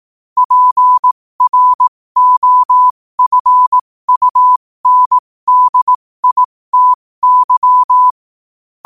Morse code, heard as a keyed tone, about 18 wpm, spaced as for 15 wpm, 1000 Hz. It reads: PROFUNDITY
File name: qotdfiles / 15wpm